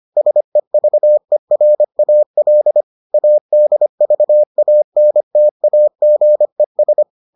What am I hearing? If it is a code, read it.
Morse code, heard as a keyed tone, about 25 wpm, 600 Hz. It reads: SEVERAL ADVANTAGES